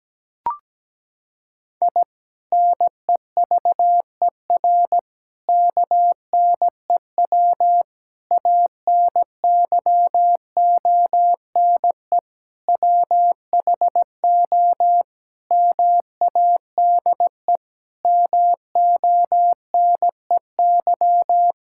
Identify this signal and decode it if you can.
Morse code, heard as a keyed tone, about 17 wpm, 700 Hz. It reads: I NEVER KNEW ANYONE WHO MADE MONEY
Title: Sentences from Top 1000 Words - Review Part 6 - 17wpm